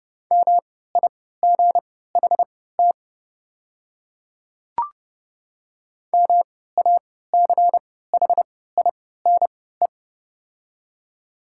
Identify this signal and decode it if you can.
Morse code, heard as a keyed tone, about 30 wpm, spaced as for 17 wpm, 690 Hz. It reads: MIGHT MACHINE
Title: 3x Character Spacing - Top 500 Words - 30wpm